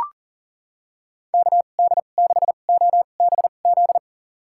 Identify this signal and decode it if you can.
Morse code, heard as a keyed tone, about 38 wpm, spaced as for 28 wpm, 700 Hz. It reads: KD6OBZ
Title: USA Call Signs - 6 Characters - 2x Character Spacing - 40wpm